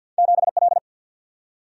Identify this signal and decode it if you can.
Morse code, heard as a keyed tone, about 49 wpm, 700 Hz. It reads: 7L
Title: Single-Letter Single-Number 50wpm